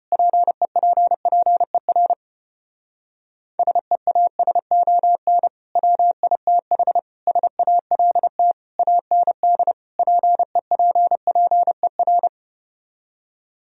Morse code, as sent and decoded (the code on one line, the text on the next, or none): .--. .--. .--. .-. / ... .- ... --- -. / .-- .. - .... / ... .- .-.. - / .- -. -.. / .--. .--. .--. .-.
PEPPER SEASON WITH SALT AND PEPPER